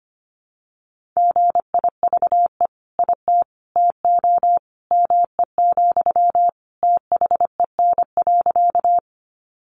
Morse code, heard as a keyed tone, about 25 wpm, 700 Hz. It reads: GIVE IT TO ME, THEN.
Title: Sentences from Top 200 Words 25wpm